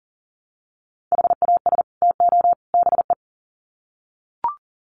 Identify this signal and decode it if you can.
Morse code, heard as a keyed tone, about 40 wpm, 700 Hz. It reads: HAS TO BE
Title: Sets of 3 Words 40wpm